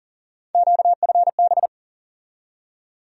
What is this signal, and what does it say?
Morse code, heard as a keyed tone, about 40 wpm, 700 Hz. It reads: QFB
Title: USA Call Sign Postfixes - 3 Letters 40wpm